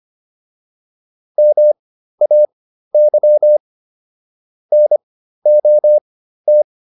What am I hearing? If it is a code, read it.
Morse code, heard as a keyed tone, about 25 wpm, spaced as for 13 wpm, 600 Hz. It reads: MAY NOT